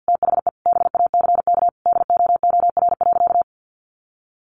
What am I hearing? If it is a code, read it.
Morse code, heard as a keyed tone, about 49 wpm, 700 Hz. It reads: THE BACK DOOR.
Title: Sentences from Top 300 Words 50wpm